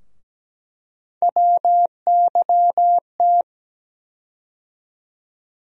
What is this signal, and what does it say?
Morse code, heard as a keyed tone, about 17 wpm, 700 Hz. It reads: WYT
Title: USA Call Sign Postfixes - 3 Letters 17wpm